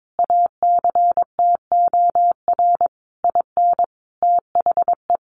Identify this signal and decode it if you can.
Morse code, heard as a keyed tone, about 22 wpm, 700 Hz. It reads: ACTOR IN THE
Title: Sentences from Top 1500 Words 22wpm